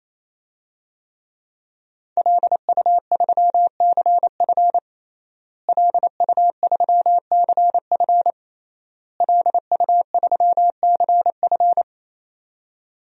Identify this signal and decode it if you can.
Morse code, heard as a keyed tone, about 28 wpm, 700 Hz. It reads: LU3CF LU3CF LU3CF
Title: DXCC-100 Call Signs - 5 Characters - Repeated 3x - 28wpm